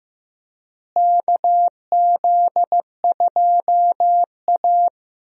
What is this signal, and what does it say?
Morse code, heard as a keyed tone, about 15 wpm, 700 Hz. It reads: KZ2A
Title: USA Call Signs - 4 Characters - Repeated 3x - 15wpm